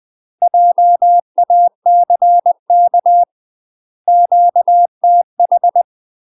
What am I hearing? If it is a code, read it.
Morse code, heard as a keyed tone, about 20 wpm, 700 Hz. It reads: JACK QTH
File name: UK MASTER QSOs TRAINING / 20WPM / UK MASTER QSOs 0031